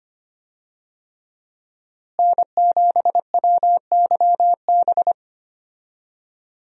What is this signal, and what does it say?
Morse code, heard as a keyed tone, about 25 wpm, 700 Hz. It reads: N7WYB
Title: USA Call Signs - 5 Characters - Repeated 3x - 25wpm